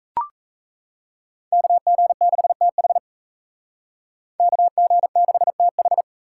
Morse code, heard as a keyed tone, about 33 wpm, 700 Hz. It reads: KG6TH KG6TH
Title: Speed Racing - USA 2x2 Call Signs 28wpm